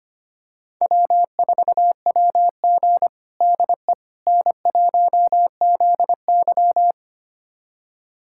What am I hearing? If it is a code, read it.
Morse code, heard as a keyed tone, about 25 wpm, 700 Hz. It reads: W4WG DE N1ZY
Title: USA Callsign DE Callsign - 4 Characters - 25wpm